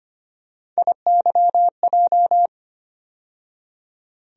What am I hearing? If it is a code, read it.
Morse code, heard as a keyed tone, about 25 wpm, 700 Hz. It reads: IYJ
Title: USA Call Sign Postfixes - 3 Letters 25wpm